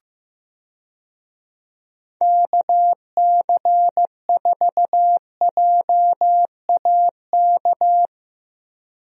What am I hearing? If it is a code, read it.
Morse code, heard as a keyed tone, about 16 wpm, 700 Hz. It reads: KC4JAK
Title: USA Call Signs - 6 Characters - 15wpm